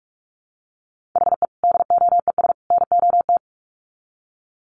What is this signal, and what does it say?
Morse code, heard as a keyed tone, about 44 wpm, 700 Hz. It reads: HE DOES NOT
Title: Sets of 3 Words 45wpm